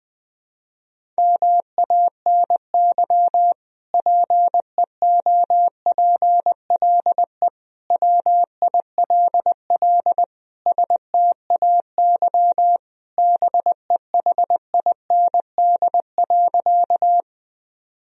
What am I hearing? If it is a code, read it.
Morse code, heard as a keyed tone, about 20 wpm, 700 Hz. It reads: MANY PEOPLE WILL STAY BEHIND.
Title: Sentences from Top 1000 Words - Review Part 1 - 20wpm